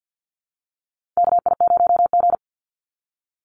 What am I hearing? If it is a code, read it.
Morse code, heard as a keyed tone, about 50 wpm, 700 Hz. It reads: KI0G
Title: NA SOTA Chaser Callsigns 50wpm